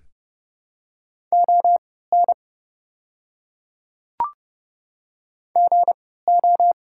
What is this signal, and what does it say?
Morse code, heard as a keyed tone, about 30 wpm, spaced as for 17 wpm, 700 Hz. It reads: ON GO
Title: 3x Character Spacing - Top 300 Words - 30wpm